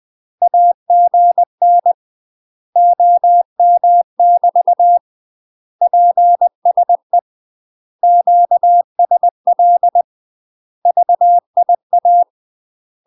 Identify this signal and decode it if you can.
Morse code, heard as a keyed tone, about 20 wpm, 700 Hz. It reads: AGN OM <BT> PSE QSL VIA
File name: UK MASTER QSOs TRAINING/20WPM